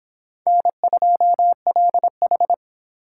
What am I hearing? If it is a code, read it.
Morse code, heard as a keyed tone, about 25 wpm, 700 Hz. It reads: N2LH